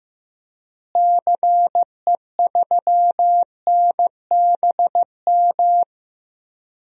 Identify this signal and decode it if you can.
Morse code, heard as a keyed tone, about 15 wpm, 700 Hz. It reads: CE3NBM
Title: DXCC-100 Call Signs - 6 Characters - Repeated 3x - 15wpm